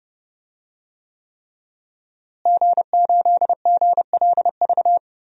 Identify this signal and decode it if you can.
Morse code, heard as a keyed tone, about 30 wpm, 700 Hz. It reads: G8GLV
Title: DXCC-100 Call Signs - 5 Characters - Repeated 3x - 30wpm